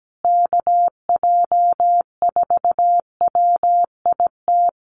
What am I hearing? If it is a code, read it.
Morse code, heard as a keyed tone, about 17 wpm, 700 Hz. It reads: KJ4WIT
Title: USA Call Signs Repeated 3x 17wpm